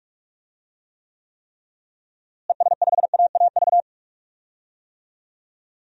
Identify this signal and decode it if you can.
Morse code, heard as a keyed tone, about 42 wpm, 700 Hz. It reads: ES5AAV
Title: DXCC-100 Call Signs - 6 Characters - Repeated 3x - 45wpm